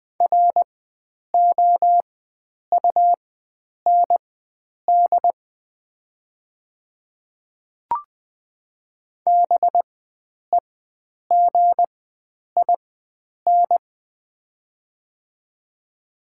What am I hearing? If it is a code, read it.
Morse code, heard as a keyed tone, about 20 wpm, spaced as for 9 wpm, 700 Hz. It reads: ROUND BEGIN
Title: 4x Character Spacing - Top 500 Words - 20wpm